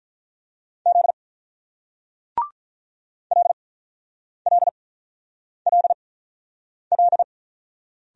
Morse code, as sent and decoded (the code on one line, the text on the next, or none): --.. / .-.. .-.. .-.. .-..
Z LLLL